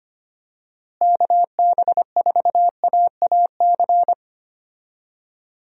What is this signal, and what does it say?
Morse code, heard as a keyed tone, about 25 wpm, 700 Hz. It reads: KB4AAC
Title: USA Call Signs - 6 Characters - Repeated 2x - 25wpm